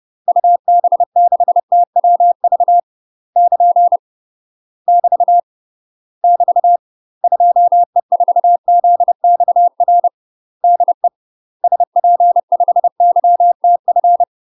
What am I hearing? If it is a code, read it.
Morse code, heard as a keyed tone, about 30 wpm, 700 Hz. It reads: UB6TWV <KN> <BT> <BT> 2E4ZXR DE SP5YTF
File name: UK MASTER QSOs TRAINING / 30WPM / UK MASTER QSOs 0048